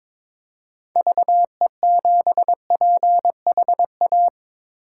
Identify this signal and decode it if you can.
Morse code, heard as a keyed tone, about 22 wpm, 700 Hz. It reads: VE7PHA
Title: K1USN SST QSO Practice - 22wpm